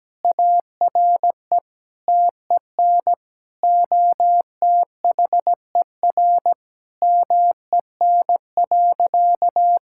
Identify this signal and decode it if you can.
Morse code, heard as a keyed tone, about 17 wpm, 700 Hz. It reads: ARE TEN OTHER MEN.
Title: Sentences from Top 1000 Words - Review Part 6 - 17wpm